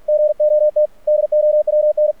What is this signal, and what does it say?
Morse code, heard as a keyed tone, about 48 wpm, 600 Hz. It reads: GOT DOWN
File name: archived / CRSnet